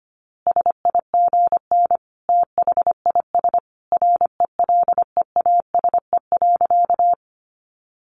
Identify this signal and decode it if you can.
Morse code, heard as a keyed tone, about 25 wpm, 700 Hz. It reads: SIGN THIS RELEASE.
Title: A Sound Of Thunder 25wpm